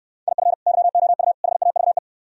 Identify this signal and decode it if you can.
Morse code, heard as a keyed tone, about 69 wpm, 700 Hz. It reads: IF YOU HAVE